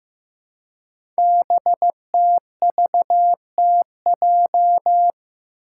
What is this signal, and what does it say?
Morse code, heard as a keyed tone, about 15 wpm, 700 Hz. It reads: BTVTJ
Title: Code Groups - 5 Letters - 15wpm